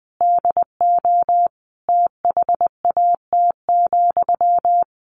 Morse code, heard as a keyed tone, about 20 wpm, 700 Hz. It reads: DO THAT,
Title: Sentences from Top 1500 Words 20wpm